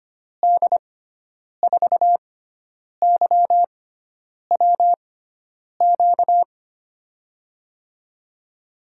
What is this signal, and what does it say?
Morse code, heard as a keyed tone, about 25 wpm, spaced as for 9 wpm, 700 Hz. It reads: D4YWQ
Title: USA Call Signs - 6 Characters - 6x Character Spacing - 25wpm